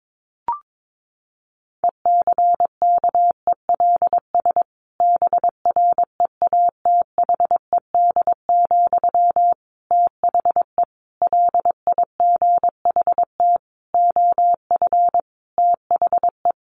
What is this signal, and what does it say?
Morse code, heard as a keyed tone, about 22 wpm, 700 Hz. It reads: ECKELS BREATHED, THE LIGHT OF THE
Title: A Sound Of Thunder 22wpm